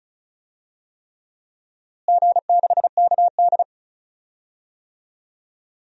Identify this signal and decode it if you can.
Morse code, heard as a keyed tone, about 36 wpm, 700 Hz. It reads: G6KD
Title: DXCC-100 Call Signs - 4 Characters - Repeated 3x - 35wpm